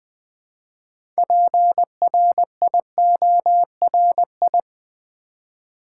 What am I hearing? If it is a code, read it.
Morse code, heard as a keyed tone, about 20 wpm, 700 Hz. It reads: PRIORI
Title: Build A Word - 3 Letters At A Time - 8-Letter Words - 20wpm